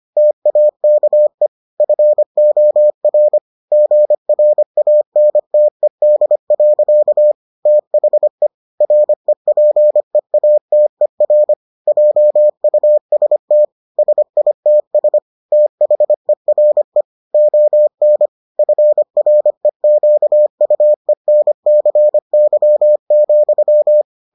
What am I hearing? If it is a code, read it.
Morse code, heard as a keyed tone, about 25 wpm, 600 Hz. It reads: TAKE FOR GRANTED. THE REPEATER JUST SITS THERE ON FREQUENCY,